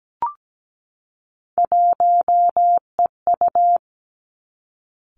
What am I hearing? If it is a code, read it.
Morse code, heard as a keyed tone, about 17 wpm, 700 Hz. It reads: 1EU
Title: Three Challenging Letter-Number Combinations 17wpm